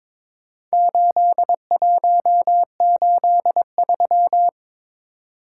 Morse code, heard as a keyed tone, about 23 wpm, 700 Hz. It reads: 8183
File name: practice / 22wpm / Four Letter-Number Combinations - 22wpm